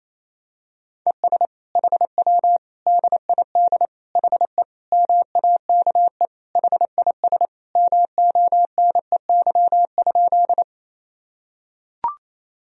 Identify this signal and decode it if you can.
Morse code, heard as a keyed tone, about 28 wpm, 700 Hz. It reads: ES HW DID HE MAKE HIS MONEY?
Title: Sentences from Top 400 Words and Abbreviations 28wpm